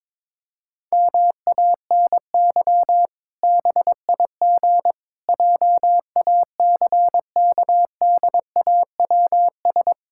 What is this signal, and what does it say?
Morse code, heard as a keyed tone, about 22 wpm, 700 Hz. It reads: MANY BIG JACKDAWS
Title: Pangram Challenge v2 - 22wpm